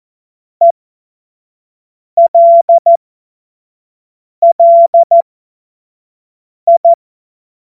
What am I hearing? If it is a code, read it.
Morse code, heard as a keyed tone, about 14 wpm, spaced as for 5 wpm, 680 Hz. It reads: ELLI
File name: Morsekurs / Witze / Witz 09 April 2024 25 F70